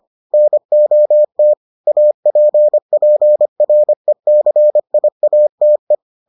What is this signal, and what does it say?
Morse code, heard as a keyed tone, about 25 wpm, 600 Hz. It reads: NOT APPRECIATE